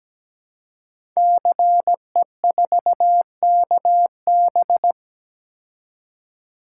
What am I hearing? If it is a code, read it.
Morse code, heard as a keyed tone, about 17 wpm, 700 Hz. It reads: CE4KB
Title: Code Groups - 5 Letter:Numbers - 17wpm